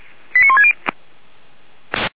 Five-tone selective call (ZVEI), heard as quick runs of five tones, tones of about 80 ms. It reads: 89160